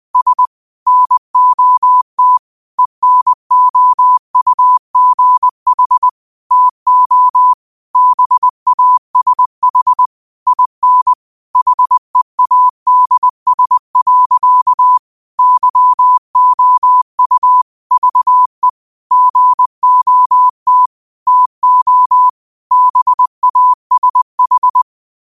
Morse code, heard as a keyed tone, about 20 wpm, 1000 Hz. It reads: S NOT ENOUGH TO BASH IN HEADS. YOU VE GOT TO BASH
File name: qotdfiles / 20wpm